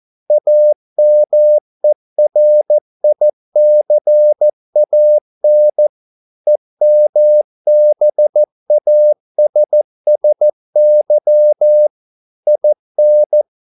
Morse code, heard as a keyed tone, about 14 wpm, 600 Hz. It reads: AMERICAN EMBASSY IN